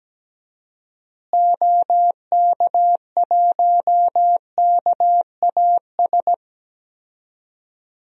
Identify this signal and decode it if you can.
Morse code, heard as a keyed tone, about 17 wpm, 700 Hz. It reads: OK1KAS
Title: DXCC-100 Call Signs - 6 Characters - Repeated 3x - 17wpm